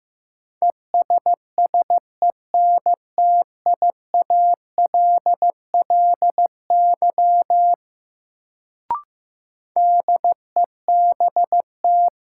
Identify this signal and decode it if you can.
Morse code, heard as a keyed tone, about 15 wpm, 700 Hz. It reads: ESSENTIALLY DEBT
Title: Sentences from Top 1300 Words - New Word Drill - 15wpm